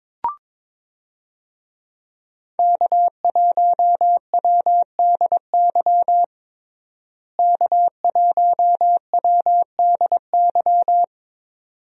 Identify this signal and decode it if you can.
Morse code, heard as a keyed tone, about 22 wpm, 700 Hz. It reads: K1WDY K1WDY
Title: USA Call Signs - 5 Characters - Repeated 2x - 22wpm